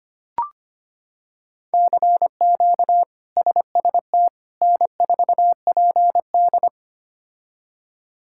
Morse code, heard as a keyed tone, about 25 wpm, 700 Hz. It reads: CQ SST N4PD
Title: K1USN SST QSO Practice - 25wpm